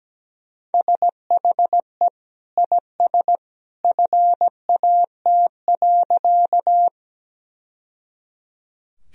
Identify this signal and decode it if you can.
Morse code, heard as a keyed tone, about 17 wpm, 700 Hz. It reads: SHE IS FAT.
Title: Sentences from Top 1000 Words - Review Part 4 - 17wpm